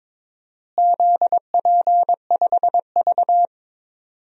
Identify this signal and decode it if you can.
Morse code, heard as a keyed tone, about 21 wpm, 700 Hz. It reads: ZP5V
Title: DXCC-100 Call Signs - 4 Characters - Repeated 3x - 22wpm